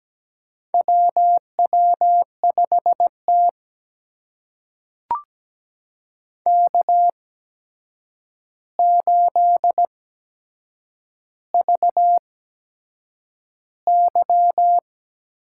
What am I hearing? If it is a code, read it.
Morse code, heard as a keyed tone, about 17 wpm, spaced as for 7 wpm, 700 Hz. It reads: WW5T K8VY
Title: USA Call Signs - 4 Characters - 8x Character Spacing - 17wpm